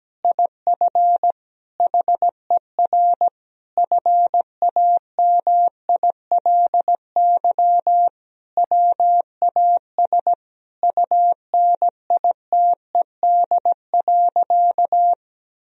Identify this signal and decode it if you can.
Morse code, heard as a keyed tone, about 17 wpm, 700 Hz. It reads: IF HER FAMILY WAS UNITED.
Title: Sentences from Top 1000 Words - Review Part 4 - 17wpm